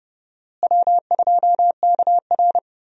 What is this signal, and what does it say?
Morse code, heard as a keyed tone, about 30 wpm, 700 Hz. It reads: W2KR
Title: USA Call Signs - 4 Characters - Repeated 2x - 30wpm